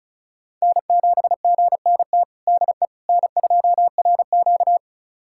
Morse code, heard as a keyed tone, about 36 wpm, 700 Hz. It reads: N7GNT DE N2RQ
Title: USA Callsign DE Callsign 35wpm